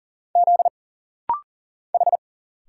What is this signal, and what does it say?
Morse code, heard as a keyed tone, about 40 wpm, 700 Hz. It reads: Z H
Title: Single Letters - Rapid-Fire 40wpm